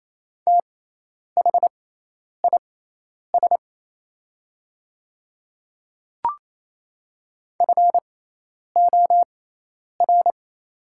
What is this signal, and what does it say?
Morse code, heard as a keyed tone, about 28 wpm, spaced as for 10 wpm, 700 Hz. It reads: THIS FOR
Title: 6x Character Spacing - Top 100 Words - 28wpm